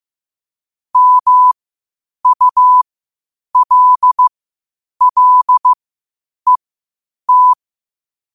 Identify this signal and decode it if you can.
Morse code, heard as a keyed tone, about 15 wpm, spaced as for 8 wpm, 1000 Hz. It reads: MULLET
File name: qotdfiles / 8wpm